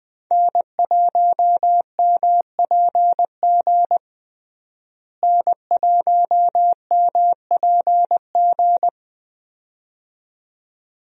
Morse code, heard as a keyed tone, about 20 wpm, 700 Hz. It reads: N1MPG N1MPG
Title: USA Call Signs - 5 Characters - Repeated 2x - 20wpm